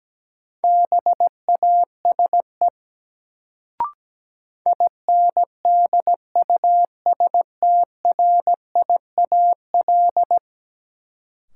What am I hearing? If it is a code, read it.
Morse code, heard as a keyed tone, about 17 wpm, 700 Hz. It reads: BASE INDUSTRIAL
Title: Sets of 5 Words - New Word Drill - 17wpm